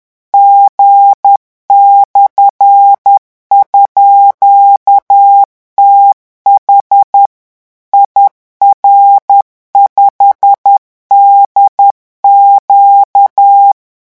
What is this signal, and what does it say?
Morse code, heard as a keyed tone, about 10 wpm, 800 Hz. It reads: G/_TH IR5DQ
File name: morse mp3 / revision13-vitesse 10 mots